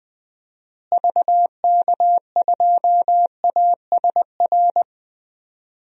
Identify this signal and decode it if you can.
Morse code, heard as a keyed tone, about 20 wpm, 700 Hz. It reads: VK2ASR
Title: DXCC-100 Call Signs - 6 Characters - Repeated 2x - 20wpm